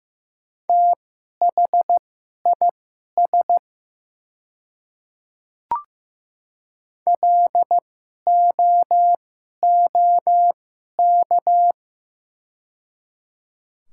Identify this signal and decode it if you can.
Morse code, heard as a keyed tone, about 15 wpm, spaced as for 11 wpm, 700 Hz. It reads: THIS LOOK
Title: 2x Character Spacing - Top 100 Words - 15wpm